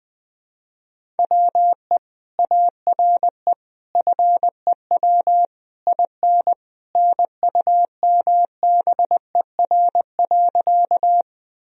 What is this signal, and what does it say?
Morse code, heard as a keyed tone, about 21 wpm, 700 Hz. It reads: WE ARE FEW IN NUMBER.
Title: Sentences from Top 1000 Words - Review Part 1 - 20wpm